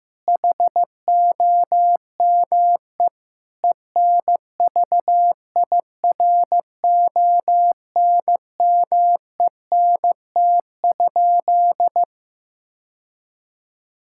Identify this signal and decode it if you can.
Morse code, heard as a keyed tone, about 15 wpm, 700 Hz. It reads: HOME ENVIRONMENT?
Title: Sentences from Top 1000 Words - Review Part 5 - 15wpm